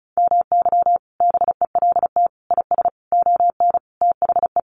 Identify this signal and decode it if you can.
Morse code, heard as a keyed tone, about 34 wpm, 700 Hz. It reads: MY BELT IS ON THE
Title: Sentences from Top 1400 Words 35wpm